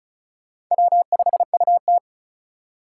Morse code, heard as a keyed tone, about 47 wpm, spaced as for 41 wpm, 700 Hz. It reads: W5UT
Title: Speed Racing - USA Call Signs 35wpm